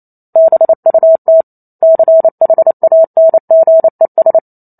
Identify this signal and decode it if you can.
Morse code, heard as a keyed tone, about 28 wpm, 650 Hz. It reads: BUT CHANGES